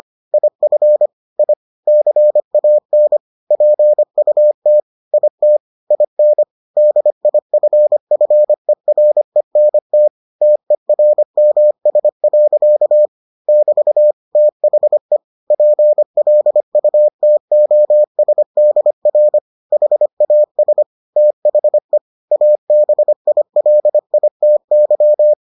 Morse code, heard as a keyed tone, about 25 wpm, 600 Hz. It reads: IF I CAN PUT IT IN DIFFERENT TERMS. <BT> THE PLUTOSDR HAS THE ABILITY